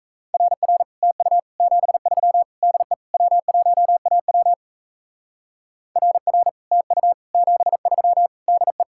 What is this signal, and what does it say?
Morse code, heard as a keyed tone, about 35 wpm, 700 Hz. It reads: RR TU 73 DE W1AW RR TU 73 DE